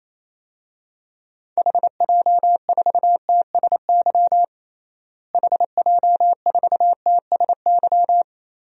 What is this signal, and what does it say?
Morse code, heard as a keyed tone, about 27 wpm, 700 Hz. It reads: HJ4TSY HJ4TSY
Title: DXCC-100 Call Signs - 6 Characters - Repeated 3x - 28wpm